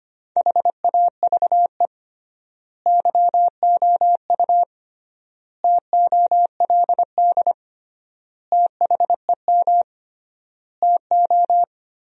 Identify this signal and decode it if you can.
Morse code, heard as a keyed tone, about 25 wpm, 700 Hz. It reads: HAVE YOU TOLD THEM TO
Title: 3x Word-Spacing - Sentences from Top 500 Words - 25wpm